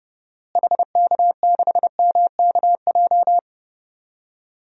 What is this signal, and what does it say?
Morse code, heard as a keyed tone, about 29 wpm, 700 Hz. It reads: HK6MKJ
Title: DXCC-100 Call Signs - 6 Characters - Repeated 3x - 30wpm